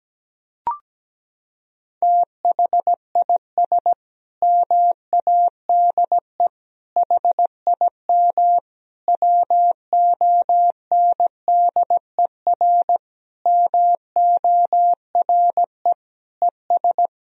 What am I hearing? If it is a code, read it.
Morse code, heard as a keyed tone, about 17 wpm, 700 Hz. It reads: THIS MADE HIM WONDER MORE ES
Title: Sentences from Top 500 Words and Abbreviations 17wpm